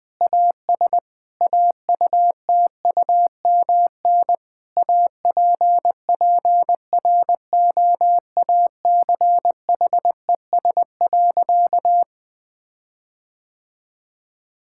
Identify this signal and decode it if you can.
Morse code, heard as a keyed tone, about 20 wpm, 700 Hz. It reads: AS AUTUMN APPROACHES.
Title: Difficult Sentences with Hints - 20wpm